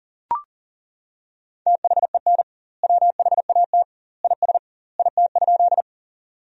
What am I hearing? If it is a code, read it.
Morse code, heard as a keyed tone, about 40 wpm, 700 Hz. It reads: THEN WHAT IS IT?
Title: 2x Word-Spacing - Sentences from Top 100 Words - 40wpm